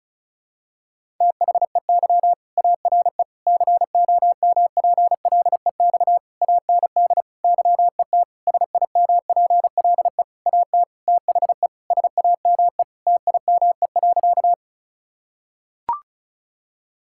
Morse code, heard as a keyed tone, about 36 wpm, 700 Hz. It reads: THEY ARE COMPLEX AND YET SIMPLE AT THE SAME TIME.
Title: Sentences from Top 1000 Words - Review Part 6 - 35wpm